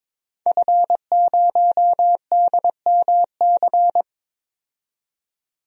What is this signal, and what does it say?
Morse code, heard as a keyed tone, about 22 wpm, 700 Hz. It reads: F0DMC
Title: DXCC-100 Call Signs - 5 Characters - 22wpm